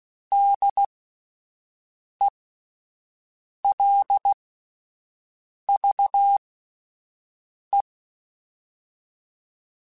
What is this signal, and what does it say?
Morse code, heard as a keyed tone, about 16 wpm, spaced as for 5 wpm, 800 Hz. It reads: DELVE